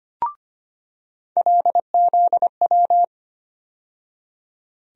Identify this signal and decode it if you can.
Morse code, heard as a keyed tone, about 25 wpm, 700 Hz. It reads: LZW
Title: USA Call Sign Postfixes - 3 Letters 25wpm